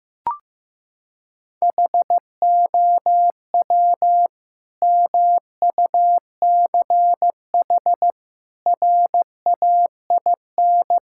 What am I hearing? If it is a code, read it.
Morse code, heard as a keyed tone, about 15 wpm, 700 Hz. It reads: HOW MUCH RAIN
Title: Sentences from Top 1000 Words - Review Part 6 - 15wpm